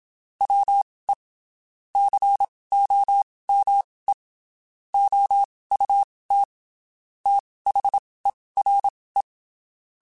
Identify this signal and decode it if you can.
Morse code, heard as a keyed tone, about 27 wpm, spaced as for 19 wpm, 800 Hz. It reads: WE COME OUT THERE
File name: (spaced) / QSOs Spaced-mp3s / WD0FBT @25 sp